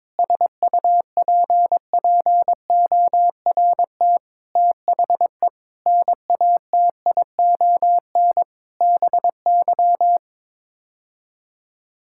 Morse code, heard as a keyed tone, about 22 wpm, 700 Hz. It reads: SUPPORT THE NATION BY
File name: Sets of 9 Words - 22wpm